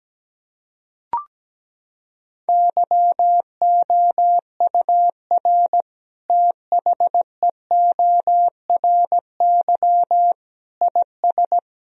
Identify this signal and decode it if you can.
Morse code, heard as a keyed tone, about 17 wpm, 700 Hz. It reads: YOUR THEORY IS
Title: Sentences from Top 1000 Words - Review Part 3 - 17wpm